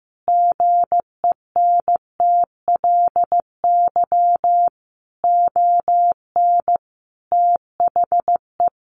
Morse code, heard as a keyed tone, about 15 wpm, 700 Hz. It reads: GENTLY ON THE
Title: Sentences from Top 1500 Words 15wpm